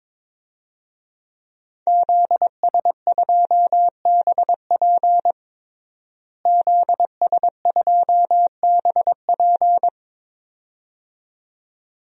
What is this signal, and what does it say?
Morse code, heard as a keyed tone, about 21 wpm, 700 Hz. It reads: ZS2BP ZS2BP
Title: DXCC-100 Call Signs - 5 Characters - Repeated 2x - 22wpm